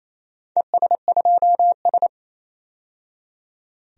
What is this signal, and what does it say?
Morse code, heard as a keyed tone, about 27 wpm, 700 Hz. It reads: ES2S
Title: DXCC-100 Call Signs - 4 Characters - 28wpm